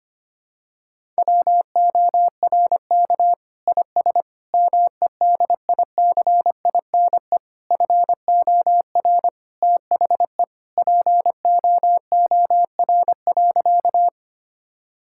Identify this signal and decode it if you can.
Morse code, heard as a keyed tone, about 25 wpm, 700 Hz. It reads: WORK IS MEDICINE FOR THE POOR.
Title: Sentences from Top 1000 Words - Review Part 6 - 25wpm